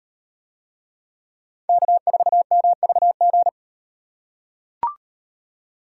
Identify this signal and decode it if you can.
Morse code, heard as a keyed tone, about 38 wpm, 700 Hz. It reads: K4MVG